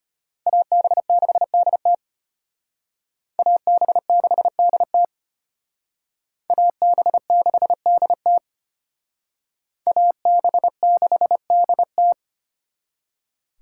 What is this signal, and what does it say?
Morse code, heard as a keyed tone, about 32 wpm, 700 Hz. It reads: AB6DT AB6DT AB6DT AB6DT